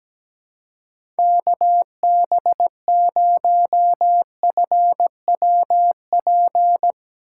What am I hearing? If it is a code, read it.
Morse code, heard as a keyed tone, about 17 wpm, 700 Hz. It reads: KB0FWP
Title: USA Call Signs - 6 Characters - 3x Character Spacing - 17wpm